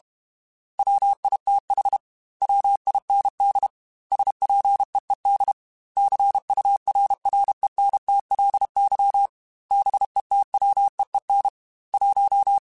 Morse code, heard as a keyed tone, about 32 wpm, 800 Hz. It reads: WITH WIND SPEED CURRENTLY BETWEEN 1
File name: Weather Sets / Weather Sets-mp3s / Weather 5 5 @30